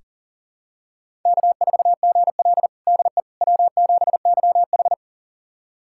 Morse code, heard as a keyed tone, about 37 wpm, 700 Hz. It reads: K4GL DE W7YH
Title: USA Callsign DE Callsign - 4 Characters - 40wpm